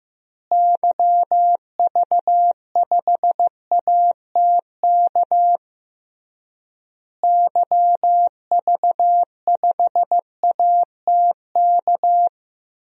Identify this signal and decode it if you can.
Morse code, heard as a keyed tone, about 15 wpm, 700 Hz. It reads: YV5ATK YV5ATK
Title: DXCC-100 Call Signs - 6 Characters - Repeated 3x - 15wpm